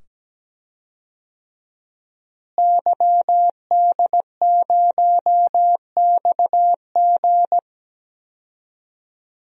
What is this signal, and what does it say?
Morse code, heard as a keyed tone, about 17 wpm, 700 Hz. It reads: YD0XG
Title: DXCC-100 Call Signs - 5 Characters - Repeated 2x - 17wpm